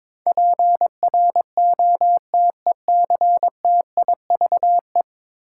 Morse code, heard as a keyed tone, about 22 wpm, 700 Hz. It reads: PROTECTIVE
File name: practice / 22wpm / 10 Letter Words - First 3 Letters Given - 22wpm